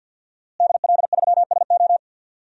Morse code, heard as a keyed tone, about 50 wpm, 700 Hz. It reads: DL3SO
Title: DXCC-100 Call Signs - 5 Characters - Repeated 3x - 50wpm